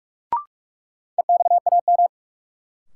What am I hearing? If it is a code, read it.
Morse code, heard as a keyed tone, about 46 wpm, 700 Hz. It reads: EXAM
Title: Sentences from Top 1000 Words - Review Part 5 - New Word Drill - 45wpm